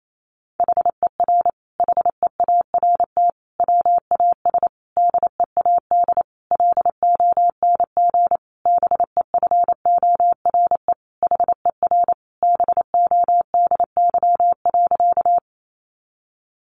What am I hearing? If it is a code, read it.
Morse code, heard as a keyed tone, about 28 wpm, 700 Hz. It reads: HER HEART WAS DEAD LONG BEFORE HER BODY.
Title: Sentences from Top 1500 Words 28wpm